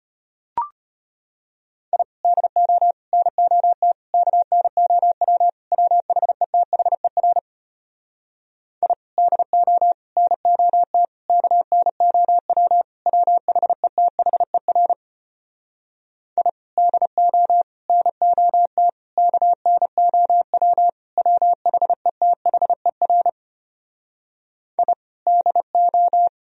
Morse code, heard as a keyed tone, about 33 wpm, 700 Hz. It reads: I DO NOT KNOW WHETHER I DO NOT KNOW WHETHER I DO NOT KNOW WHETHER I DO